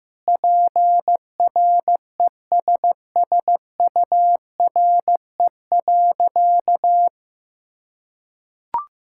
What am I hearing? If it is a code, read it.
Morse code, heard as a keyed tone, about 15 wpm, 700 Hz. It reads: PRESSURE.
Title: Sentences from Top 1000 Words - Review Part 6 - 15wpm